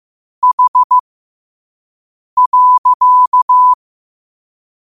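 Morse code, heard as a keyed tone, about 15 wpm, spaced as for 5 wpm, 1000 Hz. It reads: H.
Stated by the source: Morse practice quote sent at 5 wpm.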